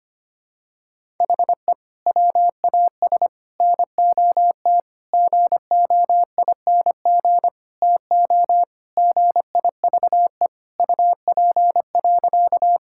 Morse code, heard as a keyed tone, about 25 wpm, 700 Hz. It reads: HE WAS NOT GOING TO GIVE UP.
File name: practice / 25wpm / Sentences with 4-Word Phrases - 25wpm